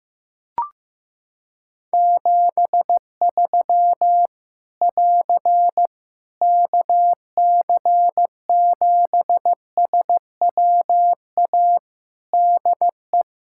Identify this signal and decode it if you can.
Morse code, heard as a keyed tone, about 15 wpm, 700 Hz. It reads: 73 <AR> KC7SWA DE
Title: Practice QSO - Exchange 4 - Ending - 15wpm